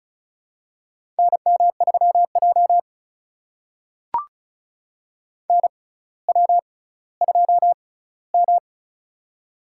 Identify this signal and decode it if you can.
Morse code, heard as a keyed tone, about 36 wpm, spaced as for 18 wpm, 700 Hz. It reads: NM3J NW2M
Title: USA Call Signs - 4 Characters - 6x Character Spacing - 35wpm